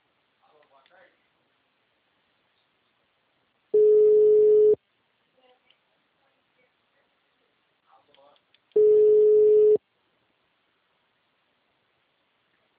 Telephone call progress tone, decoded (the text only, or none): ringback tone